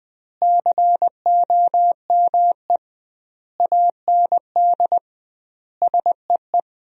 Morse code, heard as a keyed tone, about 20 wpm, 700 Hz. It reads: COME AND SEE
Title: 2x Word-Spacing - Sentences from Top 100 Words - 20wpm